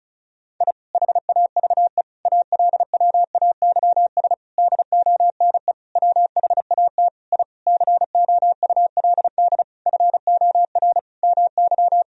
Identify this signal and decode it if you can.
Morse code, heard as a keyed tone, about 36 wpm, 700 Hz. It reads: I HAVE ALWAYS DONE WHAT I COULD FOR MY
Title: Sentences from Top 1000 Words - Review Part 2 - 35wpm